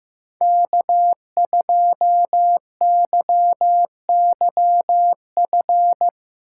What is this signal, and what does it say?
Morse code, heard as a keyed tone, about 15 wpm, 700 Hz. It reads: K2YYF
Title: USA Call Signs - 4x Character Spacing - 15wpm